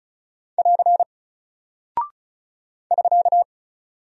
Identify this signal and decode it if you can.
Morse code, heard as a keyed tone, about 36 wpm, 700 Hz. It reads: <AR> <SK>
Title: Single Character (Letter:Number:Punctuation:Prosign) 35wpm